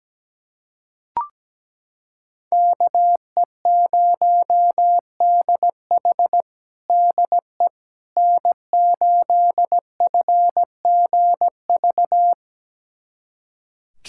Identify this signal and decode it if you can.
Morse code, heard as a keyed tone, about 17 wpm, 700 Hz. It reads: KE0DH DE N8FGV
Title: USA Callsign DE Callsign - 5 Characters - 17wpm